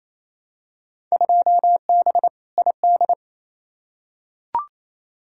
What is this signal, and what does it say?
Morse code, heard as a keyed tone, about 28 wpm, 700 Hz. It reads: 2B ID
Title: Field Day Exchanges Encore - 28wpm